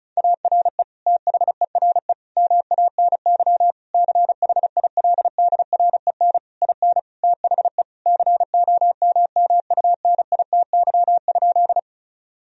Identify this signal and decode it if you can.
Morse code, heard as a keyed tone, about 36 wpm, 700 Hz. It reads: ARE THERE MANY CHILDREN IN THE COMMUNITY?
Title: Sentences from Top 1000 Words - Review Part 4 - 35wpm